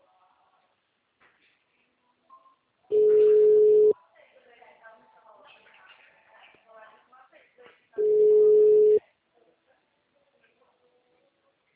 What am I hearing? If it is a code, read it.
Ringback tone (Europe, 425 Hz), with speech.